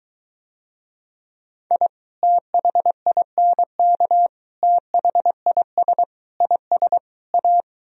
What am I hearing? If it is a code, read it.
Morse code, heard as a keyed tone, about 23 wpm, 700 Hz. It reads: I THINK THIS IS A